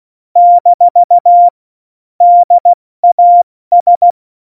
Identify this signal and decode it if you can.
Morse code, heard as a keyed tone, about 16 wpm, 700 Hz. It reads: - DAS
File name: Morsekurs / Witze / Witz 01. Dezember 2020, 70 F80